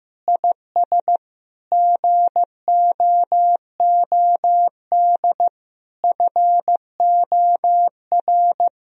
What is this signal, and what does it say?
Morse code, heard as a keyed tone, about 16 wpm, 700 Hz. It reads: IS GOOD FOR
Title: Sentences from Top 1000 Words - Review Part 3 - 15wpm